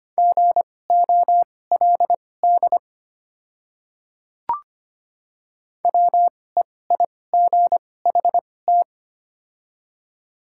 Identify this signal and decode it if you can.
Morse code, heard as a keyed tone, about 25 wpm, spaced as for 18 wpm, 700 Hz. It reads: GOLD WEIGHT
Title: 2x Character Spacing - Top 500 Words - 25wpm